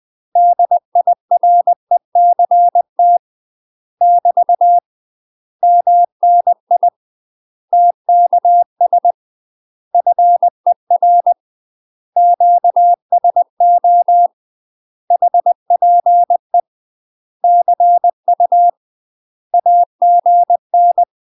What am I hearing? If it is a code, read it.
Morse code, heard as a keyed tone, about 20 wpm, 700 Hz. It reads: DIRECT <BT> MNI TKS FER QSO HPE CU AGN
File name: UK MASTER QSOs TRAINING / 20WPM / UK MASTER QSOs 0046